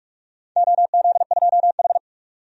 Morse code, heard as a keyed tone, about 46 wpm, 700 Hz. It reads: OZ2H
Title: DXCC-100 Call Signs - 4 Characters - 45wpm